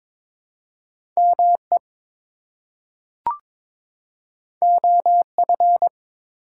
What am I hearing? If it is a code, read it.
Morse code, heard as a keyed tone, about 22 wpm, 700 Hz. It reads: ME OF
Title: Two Letter Words - 22wpm